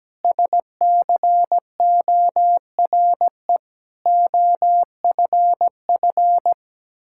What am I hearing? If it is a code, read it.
Morse code, heard as a keyed tone, about 17 wpm, 700 Hz. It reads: SCORE OFF
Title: Sentences from Top 1600 Words 17wpm